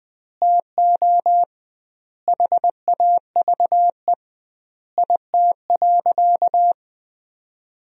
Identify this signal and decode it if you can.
Morse code, heard as a keyed tone, about 21 wpm, 700 Hz. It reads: TO HAVE IT.
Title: 2x Word-Spacing - Sentences from Top 100 Words - 20wpm